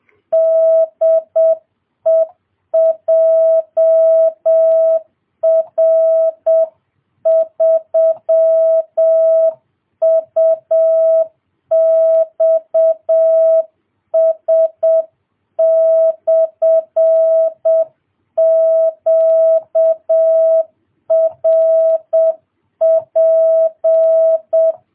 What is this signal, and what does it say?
Morse code, heard as a keyed tone, about 7 wpm, 650 Hz. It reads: DE JR3UXS/QRP